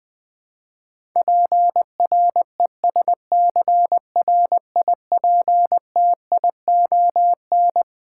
Morse code, heard as a keyed tone, about 21 wpm, 700 Hz. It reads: PRESCRIPTION
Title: Twelve Letter Words - 20wpm